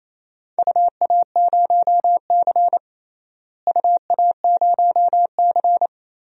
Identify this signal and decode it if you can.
Morse code, heard as a keyed tone, about 28 wpm, 700 Hz. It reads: UA0C UA0C
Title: DXCC-100 Call Signs - 4 Characters - Repeated 2x - 28wpm